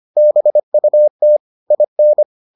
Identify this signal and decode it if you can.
Morse code, heard as a keyed tone, about 25 wpm, 600 Hz. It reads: BUT IN